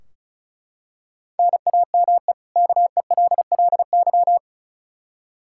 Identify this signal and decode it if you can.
Morse code, heard as a keyed tone, about 34 wpm, 700 Hz. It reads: NAME KELLY
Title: Practice QSO - Exchange 1 - Beginning - 35wpm